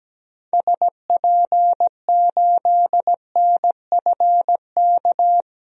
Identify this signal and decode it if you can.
Morse code, heard as a keyed tone, about 17 wpm, 700 Hz. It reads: SP8NFK
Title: DXCC-100 Call Signs - 6 Characters - 17wpm